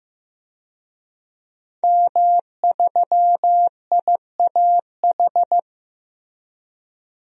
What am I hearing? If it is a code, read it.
Morse code, heard as a keyed tone, about 15 wpm, 700 Hz. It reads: M3IAH
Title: DXCC-100 Call Signs - 5 Characters - Repeated 2x - 15wpm